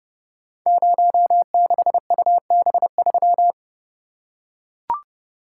Morse code, heard as a keyed tone, about 30 wpm, 690 Hz. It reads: 06UB3
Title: Code Groups - 5 Letter:Numbers - 30wpm